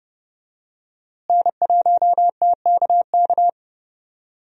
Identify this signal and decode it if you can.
Morse code, heard as a keyed tone, about 30 wpm, 700 Hz. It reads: N1TKK
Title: USA Call Signs - 5 Characters - Repeated 3x - 30wpm